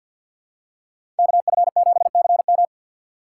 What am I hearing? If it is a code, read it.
Morse code, heard as a keyed tone, about 47 wpm, 700 Hz. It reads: KF7CM
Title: USA Call Signs - 5 Characters - 6x Character Spacing - 50wpm